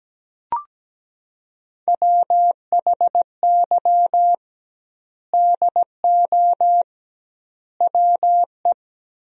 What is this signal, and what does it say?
Morse code, heard as a keyed tone, about 17 wpm, 700 Hz. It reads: WHY DO WE
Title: 2x Word-Spacing - Sentences from Top 400 Words - 17wpm